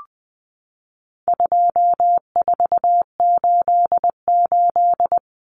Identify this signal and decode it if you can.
Morse code, heard as a keyed tone, about 20 wpm, 700 Hz. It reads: 2488
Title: Common Cut Numbers - 4-Digit 20wpm